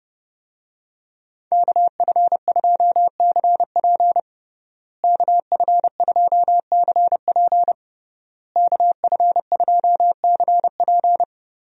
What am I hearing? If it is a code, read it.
Morse code, heard as a keyed tone, about 32 wpm, 700 Hz. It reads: KF2CP KF2CP KF2CP
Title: USA Call Signs - 5 Characters - Repeated 3x - 30wpm